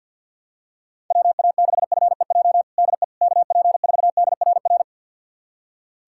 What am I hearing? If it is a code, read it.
Morse code, heard as a keyed tone, about 47 wpm, 700 Hz. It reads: WA6FEJ DE KP4DRR
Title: USA Callsign DE Callsign - 6 Characters - 50wpm